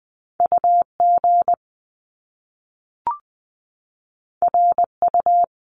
Morse code, heard as a keyed tone, about 20 wpm, 700 Hz. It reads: U G R U
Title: Top Two Letter Combinations in Words 20wpm